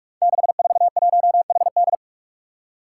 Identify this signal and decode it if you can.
Morse code, heard as a keyed tone, about 46 wpm, 700 Hz. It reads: 641HD
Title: Five Challenging Letter-Number Combinations 45wpm